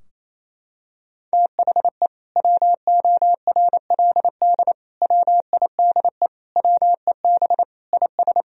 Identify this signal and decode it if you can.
Morse code, heard as a keyed tone, about 28 wpm, 700 Hz. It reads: THE WORLD WIDE WEB IS
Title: Sentences from Top 1000 Words - Review Part 4 - 28wpm